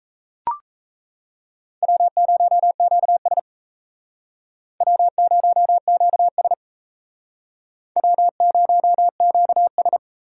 Morse code, heard as a keyed tone, about 37 wpm, 700 Hz. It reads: W0QS W0QS W0QS